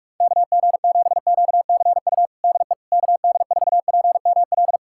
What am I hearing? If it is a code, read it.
Morse code, heard as a keyed tone, about 46 wpm, 700 Hz. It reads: KG7QCU DE KD4PML
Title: USA Callsign DE Callsign - 6 Characters - 45wpm